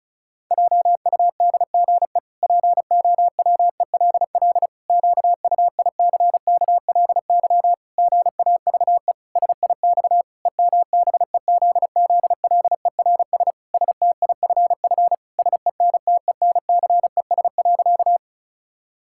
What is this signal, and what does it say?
Morse code, heard as a keyed tone, about 36 wpm, 700 Hz. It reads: JUDGE POWELL QUICKLY GAVE SIX EMBEZZLERS STIFF SENTENCES.
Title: Pangram Challenge v2 - 35wpm